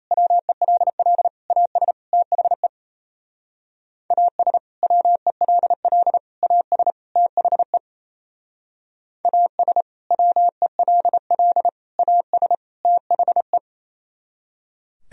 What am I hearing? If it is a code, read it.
Morse code, heard as a keyed tone, about 37 wpm, 700 Hz. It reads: WELL AS THE AS WELL AS THE AS WELL AS THE